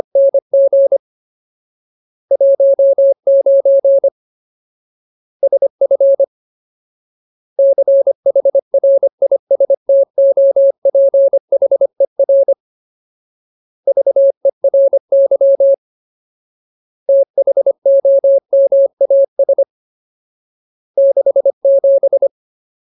Morse code, heard as a keyed tone, about 25 wpm, 550 Hz. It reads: NG 19 SF CHRISTOPHER VERY THOMAS 67
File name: Ultimate Word List / 25WPM Ultimate